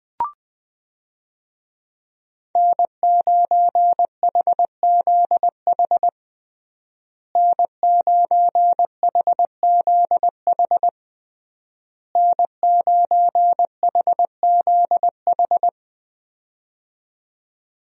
Morse code, heard as a keyed tone, about 19 wpm, 700 Hz. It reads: N9HZH N9HZH N9HZH
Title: USA Call Signs - 5 Characters - Repeated 3x - 20wpm